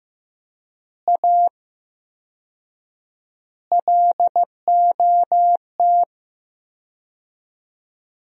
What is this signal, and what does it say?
Morse code, heard as a keyed tone, about 15 wpm, 700 Hz. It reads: A LOT
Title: 4x Word-Spacing - Sentences from Top 500 Words - 15wpm